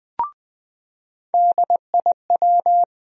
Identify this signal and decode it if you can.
Morse code, heard as a keyed tone, about 20 wpm, 700 Hz. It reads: DIW
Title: USA Call Sign Postfixes - 3 Letters 20wpm